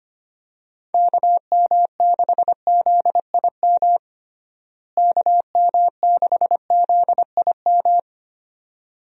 Morse code, heard as a keyed tone, about 25 wpm, 700 Hz. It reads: KM6ZIM KM6ZIM
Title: USA Call Signs - 6 Characters - Repeated 2x - 25wpm